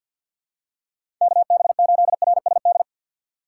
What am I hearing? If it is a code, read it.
Morse code, heard as a keyed tone, about 53 wpm, 700 Hz. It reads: K B 8 R S D
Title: USA Call Signs - 6x Character Spacing - 50wpm